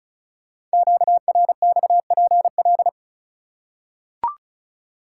Morse code, heard as a keyed tone, about 36 wpm, 690 Hz. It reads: QRXPL
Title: Code Groups - 5 Letters - 35wpm